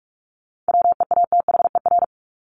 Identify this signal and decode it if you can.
Morse code, heard as a keyed tone, about 44 wpm, 700 Hz. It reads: WEATHER
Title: Top 501-750 Words 45wpm